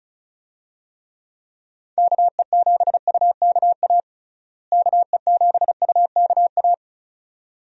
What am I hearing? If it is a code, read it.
Morse code, heard as a keyed tone, about 34 wpm, 700 Hz. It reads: KE7UKA KE7UKA
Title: USA Call Signs - 6 Characters - Repeated 2x - 35wpm